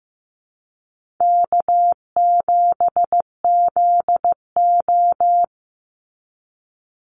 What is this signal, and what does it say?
Morse code, heard as a keyed tone, about 15 wpm, 700 Hz. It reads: K7ZO
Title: NA SOTA Chaser Callsigns 15wpm